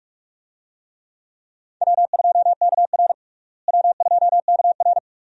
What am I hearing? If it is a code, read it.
Morse code, heard as a keyed tone, about 41 wpm, 700 Hz. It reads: W2KR W2KR
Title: USA Call Signs - 4 Characters - Repeated 2x - 45wpm